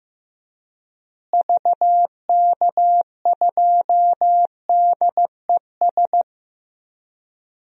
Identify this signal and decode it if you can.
Morse code, heard as a keyed tone, about 15 wpm, 700 Hz. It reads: VK2DES
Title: DXCC-100 Call Signs - 6 Characters - Repeated 2x - 15wpm